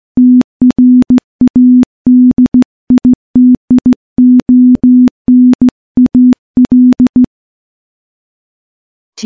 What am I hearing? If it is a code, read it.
Morse code, heard as a keyed tone, about 15 wpm, 250 Hz. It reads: TRADITIONAL
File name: morse / morse-training-15WPM